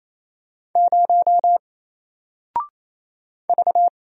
Morse code, heard as a keyed tone, about 27 wpm, 700 Hz. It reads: 0 V
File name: Single-Letter-Number-28wpm